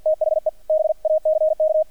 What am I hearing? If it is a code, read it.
Morse code, heard as a keyed tone, about 50 wpm, 610 Hz, with radio noise. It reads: THE BACK
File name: archived / CRSnet